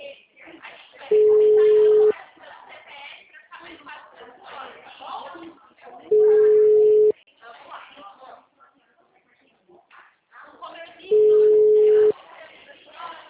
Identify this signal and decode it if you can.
Ringback tone (Europe, 425 Hz), with speech.
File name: out- -2003- - - .55149